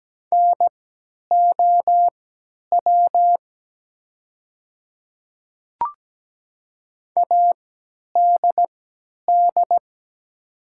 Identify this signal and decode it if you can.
Morse code, heard as a keyed tone, about 17 wpm, spaced as for 10 wpm, 700 Hz. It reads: NOW ADD
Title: 3x Character Spacing - Top 200 Words - 17wpm